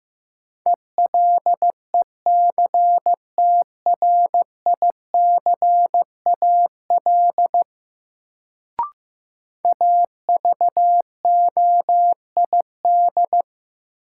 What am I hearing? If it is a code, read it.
Morse code, heard as a keyed tone, about 15 wpm, 700 Hz. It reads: ELECTRICAL AVOID
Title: Sentences from Top 1000 Words - Review Part 6 - New Word Drill - 15wpm